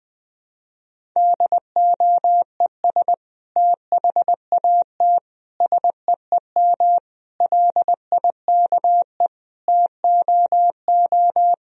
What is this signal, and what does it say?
Morse code, heard as a keyed tone, about 20 wpm, 700 Hz. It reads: DOES THAT SEEM LIKE TOO
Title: Sentences from Top 1000 Words - Review Part 1 - 20wpm